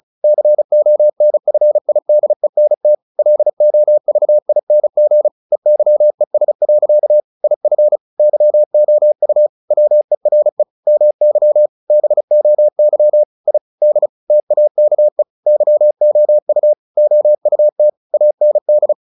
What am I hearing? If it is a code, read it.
Morse code, heard as a keyed tone, about 34 wpm, 600 Hz. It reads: CONFIDENT LOVING EYES. IF YOU WERE MY BOY I D TAKE YOU OUT AND
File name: the old man and the sea / the old man and the sea 1 35wpm